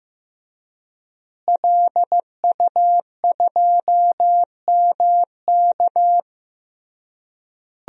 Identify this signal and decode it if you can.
Morse code, heard as a keyed tone, about 16 wpm, 700 Hz. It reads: LU2MK
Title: DXCC-100 Call Signs - 5 Characters - Repeated 3x - 15wpm